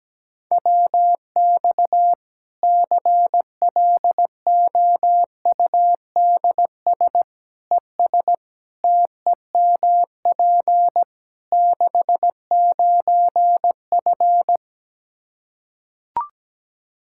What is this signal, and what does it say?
Morse code, heard as a keyed tone, about 17 wpm, 700 Hz. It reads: WX CLOUDS ES TEMP 69F
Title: K1USN SST QSO Practice 17wpm